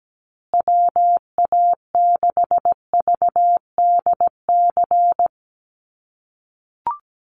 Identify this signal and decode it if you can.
Morse code, heard as a keyed tone, about 17 wpm, 700 Hz. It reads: WA6VDC
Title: USA Call Signs Repeated 2x 17wpm